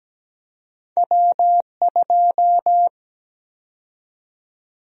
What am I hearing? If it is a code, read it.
Morse code, heard as a keyed tone, about 17 wpm, 700 Hz. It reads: W2
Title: USA Call Sign Prefixes - 17wpm